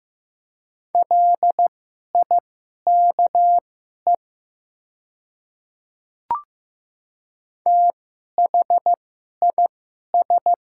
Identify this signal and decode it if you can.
Morse code, heard as a keyed tone, about 15 wpm, spaced as for 11 wpm, 700 Hz. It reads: LIKE THIS
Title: 2x Character Spacing - Top 100 Words - 15wpm